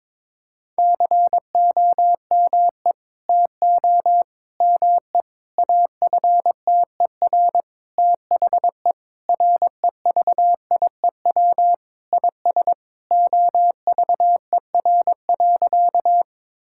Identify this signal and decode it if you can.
Morse code, heard as a keyed tone, about 22 wpm, 700 Hz. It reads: COME TO ME AFTER THE REVIEW IS OVER.
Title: Sentences from Top 1000 Words - Review Part 6 - 22wpm